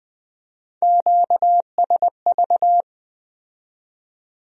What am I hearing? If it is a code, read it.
Morse code, heard as a keyed tone, about 20 wpm, 700 Hz. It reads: QSV
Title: Speed-Racing Q-Codes - 17wpm